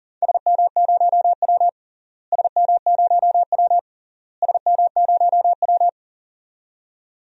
Morse code, heard as a keyed tone, about 40 wpm, 700 Hz. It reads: SM0W SM0W SM0W
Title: DXCC-100 Call Signs - 4 Characters - Repeated 3x - 40wpm